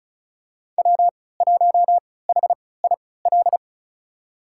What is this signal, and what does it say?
Morse code, heard as a keyed tone, about 34 wpm, spaced as for 20 wpm, 700 Hz. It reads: W1HIL
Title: USA Call Signs - 5 Characters - 3x Character Spacing - 35wpm